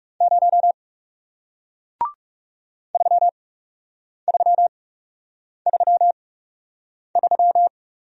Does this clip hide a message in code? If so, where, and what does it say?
Yes: Morse code from 0.2 to 7.7 s, heard as a keyed tone, about 40 wpm, spaced as for 8 wpm, 700 Hz. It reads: 0 3333